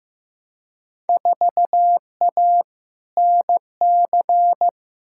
Morse code, heard as a keyed tone, about 15 wpm, 700 Hz. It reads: 4A NC
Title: Field Day Exchanges Encore - 15wpm